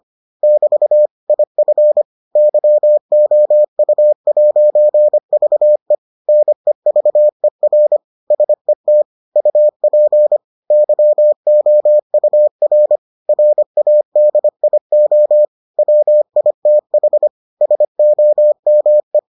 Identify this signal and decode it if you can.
Morse code, heard as a keyed tone, about 25 wpm, 600 Hz. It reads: <BT> IF YOU'VE NEVER SET UP YOUR RADIO WITH SOME